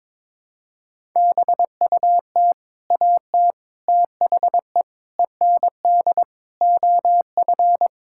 Morse code, heard as a keyed tone, about 22 wpm, 700 Hz. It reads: BUT AT THE END OF
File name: practice / 22wpm / Sets of 8 Words - 22wpm